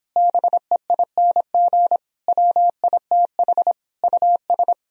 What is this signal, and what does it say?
Morse code, heard as a keyed tone, about 31 wpm, 700 Hz. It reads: BEING WITH US